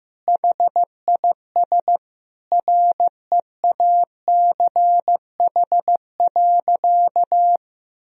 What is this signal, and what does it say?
Morse code, heard as a keyed tone, about 15 wpm, 700 Hz. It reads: HIS REACH.
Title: Sentences from Top 1000 Words - Review Part 4 - 15wpm